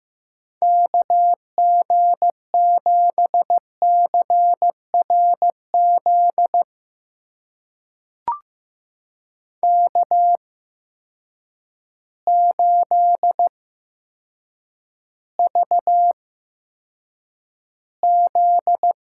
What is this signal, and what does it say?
Morse code, heard as a keyed tone, about 15 wpm, spaced as for 8 wpm, 700 Hz. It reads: KG7CRZ K8VZ
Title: USA Call Signs - 8x Character Spacing - 15wpm